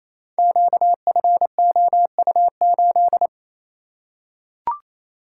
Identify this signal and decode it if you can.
Morse code, heard as a keyed tone, about 29 wpm, 700 Hz. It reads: QFOU8
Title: Code Groups - 5 Letter:Numbers - 28wpm